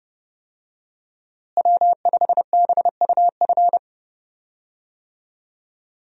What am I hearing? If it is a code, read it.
Morse code, heard as a keyed tone, about 30 wpm, 700 Hz. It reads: W5BUF
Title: USA Call Signs - 5 Characters - Repeated 3x - 30wpm